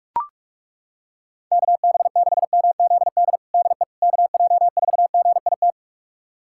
Morse code, heard as a keyed tone, about 41 wpm, 700 Hz. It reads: KB6MZD DE KJ4GIT
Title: USA Callsign DE Callsign - 6 Characters - 45wpm